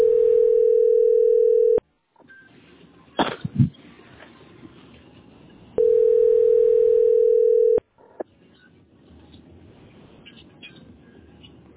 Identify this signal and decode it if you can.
Ringback tone (North America), 440 + 480 Hz.